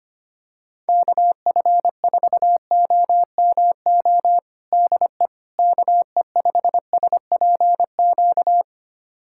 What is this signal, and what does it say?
Morse code, heard as a keyed tone, about 25 wpm, 700 Hz. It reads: KF4OMO DE KE5SPQ
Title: USA Callsign DE Callsign - 6 Characters - 25wpm